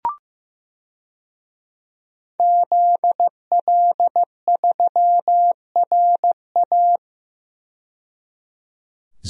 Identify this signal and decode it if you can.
Morse code, heard as a keyed tone, about 15 wpm, 700 Hz. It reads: ZL3RA
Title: DXCC-100 Call Signs - 5 Characters - 15wpm